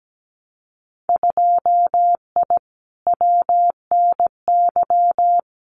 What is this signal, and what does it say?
Morse code, heard as a keyed tone, about 17 wpm, 700 Hz. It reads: 2I WNY
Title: Winter Field Day Exchanges Encore 17wpm